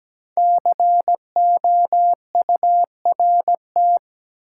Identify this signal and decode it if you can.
Morse code, heard as a keyed tone, about 17 wpm, 700 Hz. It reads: COURT
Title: Sentences from Top 1600 Words - New Word Drill - 17wpm